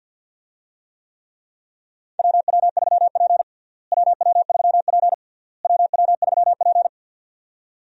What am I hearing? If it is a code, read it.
Morse code, heard as a keyed tone, about 51 wpm, 700 Hz. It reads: WW3P WW3P WW3P
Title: USA Call Signs - 4 Characters - Repeated 3x - 50wpm